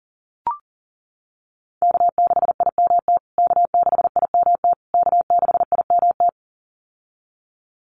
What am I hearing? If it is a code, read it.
Morse code, heard as a keyed tone, about 40 wpm, 700 Hz. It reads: K6IMT K6IMT K6IMT
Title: USA Call Signs Repeated 3x 40wpm